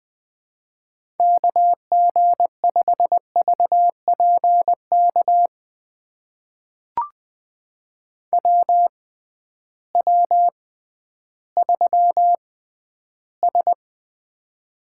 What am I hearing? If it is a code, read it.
Morse code, heard as a keyed tone, about 20 wpm, spaced as for 12 wpm, 700 Hz. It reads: KG5VPK WW3S
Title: USA Call Signs - 6x Character Spacing - 20wpm